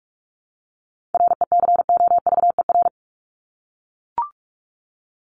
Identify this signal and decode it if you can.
Morse code, heard as a keyed tone, about 44 wpm, 700 Hz. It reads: RECOVER
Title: Top 1751-2000 Words 45wpm